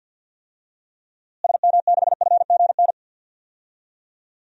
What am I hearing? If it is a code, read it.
Morse code, heard as a keyed tone, about 52 wpm, 700 Hz. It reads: SM6FGN
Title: DXCC-100 Call Signs - 6 Characters - Repeated 3x - 50wpm